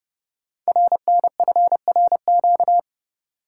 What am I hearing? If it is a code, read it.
Morse code, heard as a keyed tone, about 30 wpm, 700 Hz. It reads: RNFRQ
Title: Code Groups - 5 Letters - 30wpm